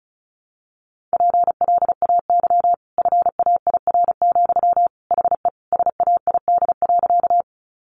Morse code, heard as a keyed tone, about 34 wpm, 700 Hz. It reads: PLAY FAIR, HE SAID.
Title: Sentences from Top 1300 Words 35wpm